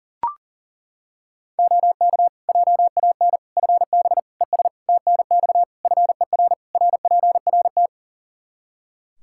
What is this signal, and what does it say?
Morse code, heard as a keyed tone, about 40 wpm, 700 Hz. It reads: OK JAN FB ES TNX FER RPRT
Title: K1USN SST QSO Practice 40wpm